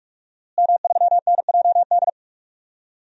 Morse code, heard as a keyed tone, about 46 wpm, 700 Hz. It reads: M3NJD
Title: DXCC-100 Call Signs - 5 Characters - Repeated 2x - 45wpm